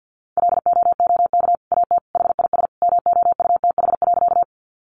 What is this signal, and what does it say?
Morse code, heard as a keyed tone, about 50 wpm, 700 Hz. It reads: LOOK AT HIS MOUTH.
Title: Sentences from Top 1500 Words 50wpm